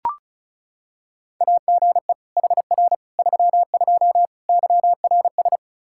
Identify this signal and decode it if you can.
Morse code, heard as a keyed tone, about 34 wpm, 700 Hz. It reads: AGE HR 32 YRS
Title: Practice QSO - Exchange 3 - More Info - 35wpm